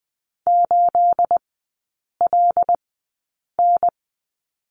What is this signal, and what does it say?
Morse code, heard as a keyed tone, about 20 wpm, spaced as for 8 wpm, 690 Hz. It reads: 8LN
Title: 212 Lesson 34e - Number 8 - Three Characters 20wpm